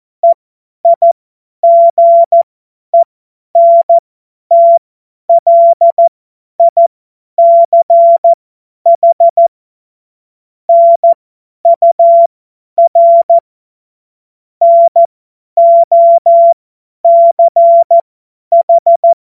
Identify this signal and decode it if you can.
Morse code, heard as a keyed tone, about 14 wpm, spaced as for 10 wpm, 680 Hz. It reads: EIGENTLICH NUR NOCH
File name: Morsekurs / Witze / Witz 20. Juni 2023 50 F70